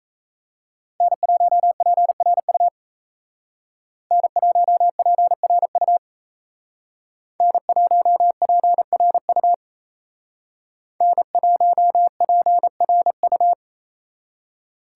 Morse code, heard as a keyed tone, about 32 wpm, 700 Hz. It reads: N1PRU N1PRU N1PRU N1PRU
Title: Speed Racing - USA 1x3 Call Signs 28wpm